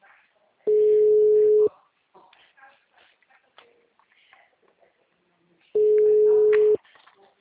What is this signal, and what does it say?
Ringback tone (Europe, 425 Hz), with speech.
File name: out- -2014- - - .40579